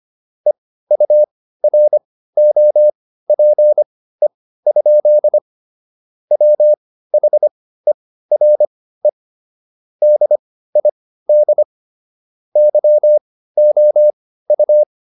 Morse code, heard as a keyed tone, about 25 wpm, spaced as for 15 wpm, 600 Hz. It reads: EUROPE? WHERE DID YOU